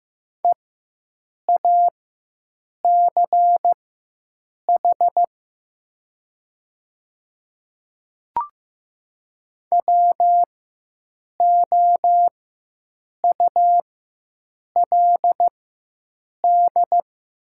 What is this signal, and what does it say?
Morse code, heard as a keyed tone, about 15 wpm, spaced as for 7 wpm, 700 Hz. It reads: EACH WOULD
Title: 4x Character Spacing - Top 100 Words - 15wpm